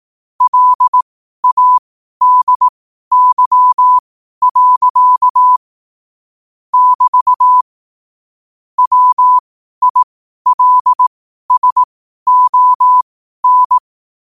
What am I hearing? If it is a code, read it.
Morse code, heard as a keyed tone, about 18 wpm, spaced as for 12 wpm, 1000 Hz. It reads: LADY. <BT> WILSON
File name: qotdfiles / 12wpm